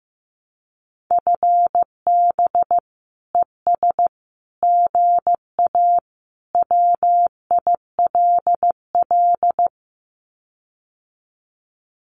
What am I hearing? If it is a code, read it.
Morse code, heard as a keyed tone, about 15 wpm, 700 Hz. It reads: FB ES GA WILL
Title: Common QSO Phrases Encore 15wpm